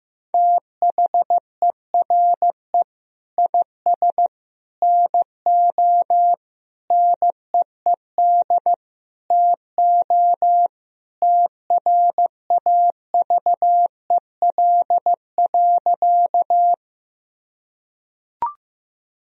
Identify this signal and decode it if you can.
Morse code, heard as a keyed tone, about 15 wpm, 700 Hz. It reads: THERE IS NO NEED TO TRAVEL.
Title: Sentences from Top 1000 Words - Review Part 2 - 15wpm